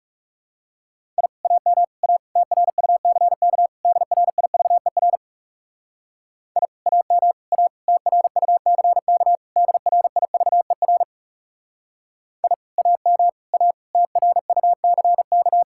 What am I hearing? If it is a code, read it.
Morse code, heard as a keyed tone, about 37 wpm, 700 Hz. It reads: I AM A TRUCK DRIVER I AM A TRUCK DRIVER I AM A TRUCK
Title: Speed-Racing Common QSO Phrases - 30wpm